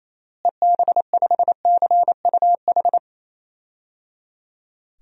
Morse code, heard as a keyed tone, about 28 wpm, 700 Hz. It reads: EB5CUH
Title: DXCC-100 Call Signs - 6 Characters - Repeated 3x - 28wpm